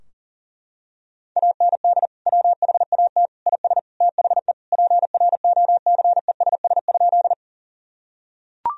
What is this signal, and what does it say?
Morse code, heard as a keyed tone, about 40 wpm, 700 Hz. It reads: AND WHAT IS THE PROCESS?
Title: Sentences from Top 1000 Words - Review Part 3 - 40wpm